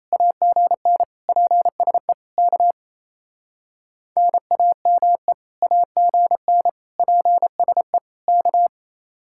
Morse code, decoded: AGN PSE K NAME AGN PSE K

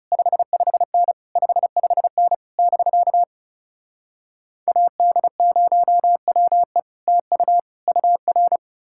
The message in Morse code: 55N 55N <BK> AD0WE TU UR

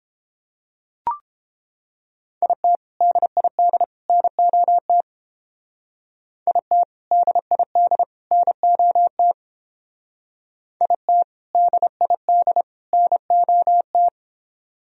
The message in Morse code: IT DID NOT IT DID NOT IT DID NOT

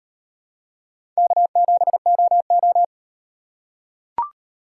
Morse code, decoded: K7OO

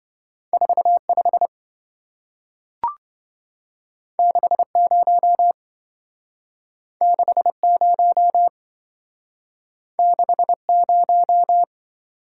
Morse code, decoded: 45 60 60 60

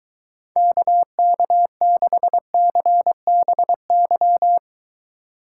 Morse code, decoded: KK6CBY

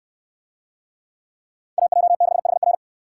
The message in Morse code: AJ6HR

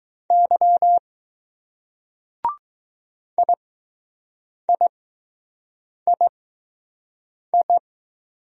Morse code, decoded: Y IIII

